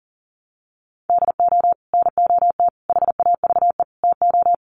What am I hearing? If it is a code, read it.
Morse code, heard as a keyed tone, about 40 wpm, 700 Hz. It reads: DO NOT HAVE TO
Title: Sets of 5 Words 40wpm